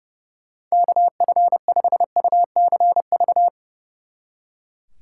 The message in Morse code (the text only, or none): KF5UCV